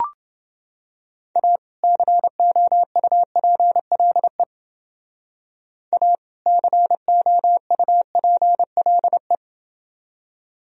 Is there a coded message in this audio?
Yes: Morse code from 1.4 to 9.4 s, heard as a keyed tone, about 29 wpm, 700 Hz. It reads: A COUPLE A COUPLE